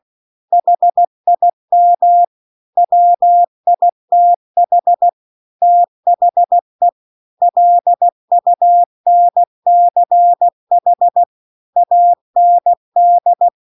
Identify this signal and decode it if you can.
Morse code, heard as a keyed tone, about 16 wpm, 700 Hz. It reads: HIM WITH THE LUNCH AND